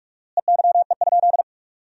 Morse code, heard as a keyed tone, about 47 wpm, 700 Hz. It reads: EYE?
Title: 4x Word-Spacing - Sentences from Top 300 Words - 45wpm